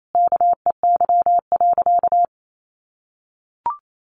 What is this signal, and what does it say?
Morse code, heard as a keyed tone, about 28 wpm, 700 Hz. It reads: KEY.
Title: Sentences from Top 1400 Words 28wpm